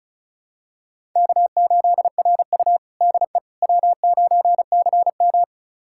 Morse code, heard as a keyed tone, about 36 wpm, 700 Hz. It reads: K8RU DE W9CM
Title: USA Callsign DE Callsign - 4 Characters - 35wpm